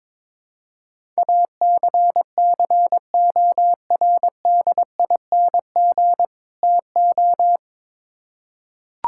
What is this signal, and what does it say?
Morse code, heard as a keyed tone, about 22 wpm, 700 Hz. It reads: ACCORDING TO
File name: Build A Phrase - 2-Word Phrases - 22wpm